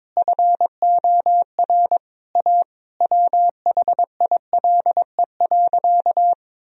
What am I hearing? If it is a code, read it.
Morse code, heard as a keyed tone, about 22 wpm, 700 Hz. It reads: FOR A WHILE.
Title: Sentences from Top 1000 Words - Review Part 1 - 22wpm